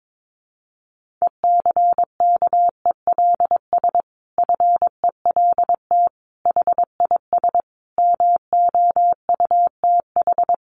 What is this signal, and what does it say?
Morse code, heard as a keyed tone, about 22 wpm, 700 Hz. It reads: ECKELS FELT HIS MOUTH
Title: A Sound Of Thunder 22wpm